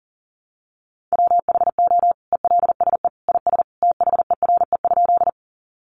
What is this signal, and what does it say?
Morse code, heard as a keyed tone, about 40 wpm, 700 Hz. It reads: WHO ELSE IS THERE?
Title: Sentences from Top 800 Words 40wpm